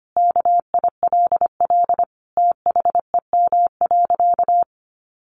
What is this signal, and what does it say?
Morse code, heard as a keyed tone, about 25 wpm, 700 Hz. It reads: KILL THEM.
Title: Easy Phrases 25wpm